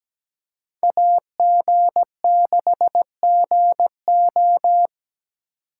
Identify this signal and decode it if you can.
Morse code, heard as a keyed tone, about 20 wpm, 700 Hz. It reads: AG6GO
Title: Speed Racing - USA Call Signs 17wpm